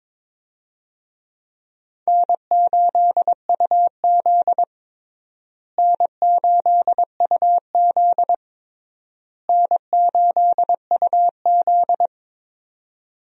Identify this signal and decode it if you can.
Morse code, heard as a keyed tone, about 22 wpm, 700 Hz. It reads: N8UZ N8UZ N8UZ
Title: USA Call Signs - 4 Characters - Repeated 3x - 22wpm